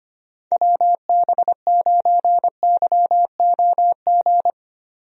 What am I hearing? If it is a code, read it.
Morse code, heard as a keyed tone, about 25 wpm, 700 Hz. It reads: WB9YOG
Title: USA Call Signs - 6 Characters - 25wpm